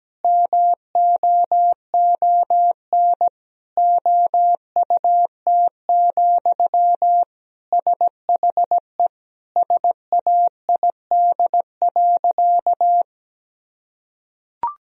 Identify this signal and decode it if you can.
Morse code, heard as a keyed tone, about 17 wpm, 700 Hz. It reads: MOON OUT, SHE SAID.
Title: Sentences from Top 1000 Words - Review Part 3 - 17wpm